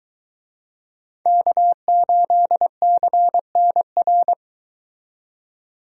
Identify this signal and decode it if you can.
Morse code, heard as a keyed tone, about 22 wpm, 700 Hz. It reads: K8CNR